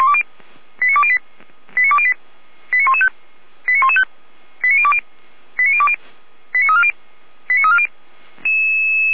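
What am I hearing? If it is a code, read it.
Five-tone selective call (ZVEI), heard as quick runs of five tones, tones of about 80 ms. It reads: 89208 89208 89115 89115 89020 89020 89340 89340
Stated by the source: ZVEI five-tone selective call received on SDR.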